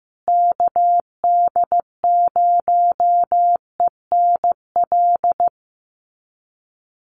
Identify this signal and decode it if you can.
Morse code, heard as a keyed tone, about 15 wpm, 700 Hz. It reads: KD0ENL
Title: USA Call Signs Repeated 3x 15wpm